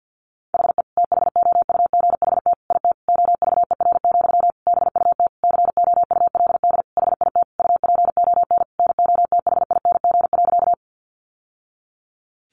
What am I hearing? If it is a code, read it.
Morse code, heard as a keyed tone, about 50 wpm, 700 Hz. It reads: HE THOUGHT IT OVER, BUT COULD HIT UPON NOTHING.
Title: Sentences from Top 1200 Words 50wpm